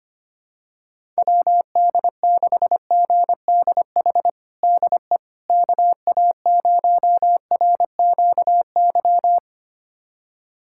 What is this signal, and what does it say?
Morse code, heard as a keyed tone, about 25 wpm, 700 Hz. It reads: WD6GDH DE KA0RQY
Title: USA Callsign DE Callsign - 6 Characters - 25wpm